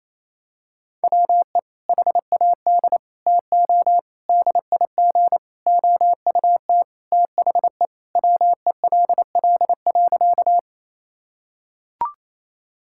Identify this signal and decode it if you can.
Morse code, heard as a keyed tone, about 28 wpm, 700 Hz. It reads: WE HAD TO DIG OUT THE WELL.
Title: Sentences from Top 1600 Words 28wpm